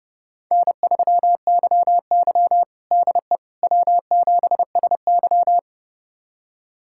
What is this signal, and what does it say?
Morse code, heard as a keyed tone, about 31 wpm, 700 Hz. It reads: N3YY DE W7SY
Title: USA Callsign DE Callsign - 4 Characters - 30wpm